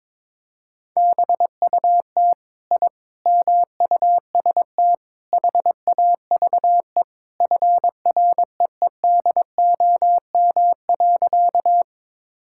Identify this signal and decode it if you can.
Morse code, heard as a keyed tone, about 21 wpm, 700 Hz. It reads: BUT I MUST HAVE FREEDOM.
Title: Sentences from Top 1000 Words - Review Part 5 - 22wpm